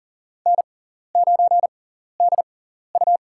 Morse code, heard as a keyed tone, about 40 wpm, spaced as for 14 wpm, 700 Hz. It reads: N9DU
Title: USA Call Signs - 4 Characters - 6x Character Spacing - 40wpm